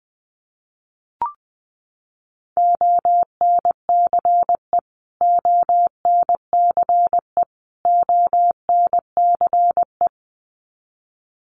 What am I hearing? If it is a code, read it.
Morse code, heard as a keyed tone, about 20 wpm, 700 Hz. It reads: ONCE ONCE ONCE
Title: Top 500 Words Repeated 20wpm